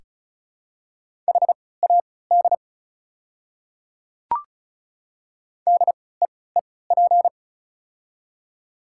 Morse code, heard as a keyed tone, about 36 wpm, spaced as for 20 wpm, 690 Hz. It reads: HAD DEEP
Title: 3x Character Spacing - Top 500 Words - 35wpm